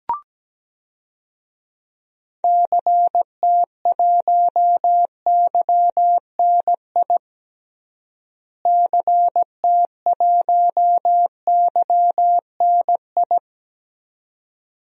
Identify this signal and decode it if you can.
Morse code, heard as a keyed tone, about 17 wpm, 700 Hz. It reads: CT1YNI CT1YNI
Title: DXCC-100 Call Signs - 6 Characters - Repeated 2x - 17wpm